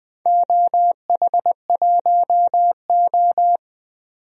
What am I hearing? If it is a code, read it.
Morse code, heard as a keyed tone, about 20 wpm, 700 Hz. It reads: OH1O
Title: DXCC-100 Call Signs - 4 Characters - 20wpm